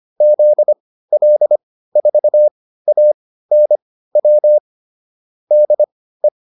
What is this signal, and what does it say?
Morse code, heard as a keyed tone, about 25 wpm, spaced as for 15 wpm, 600 Hz. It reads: ZL4ANW DE
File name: QSO404 15